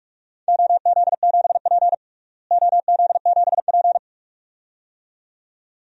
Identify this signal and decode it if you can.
Morse code, heard as a keyed tone, about 42 wpm, 700 Hz. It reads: OZ7P OZ7P
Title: DXCC-100 Call Signs - 4 Characters - Repeated 2x - 45wpm